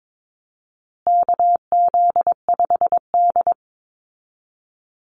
Morse code, heard as a keyed tone, about 22 wpm, 700 Hz. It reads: KZ5D
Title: Call Signs Repeated 2x - 22wpm